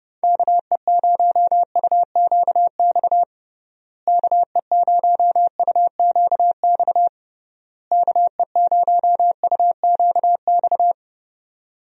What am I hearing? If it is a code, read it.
Morse code, heard as a keyed tone, about 30 wpm, 700 Hz. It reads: KE0UQX KE0UQX KE0UQX
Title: USA Call Signs - 6 Characters - Repeated 3x - 30wpm